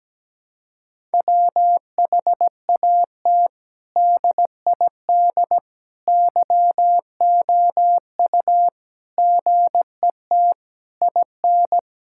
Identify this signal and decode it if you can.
Morse code, heard as a keyed tone, about 17 wpm, 700 Hz. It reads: WHAT DID YOU GET IN
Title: Sentences from Top 1000 Words - Review Part 6 - 17wpm